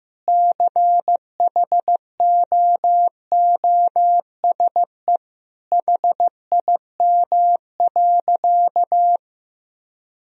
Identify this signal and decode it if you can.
Morse code, heard as a keyed tone, about 15 wpm, 700 Hz. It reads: CHOOSE HIM.
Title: Sentences from Top 1000 Words - Review Part 4 - 15wpm